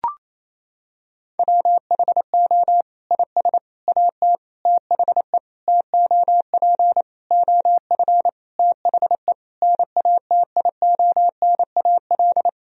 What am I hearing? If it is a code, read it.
Morse code, heard as a keyed tone, about 28 wpm, 700 Hz. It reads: WHO IS AT THE TOP OF THE NATIONAL
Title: Sentences from Top 1000 Words - Review Part 4 - 28wpm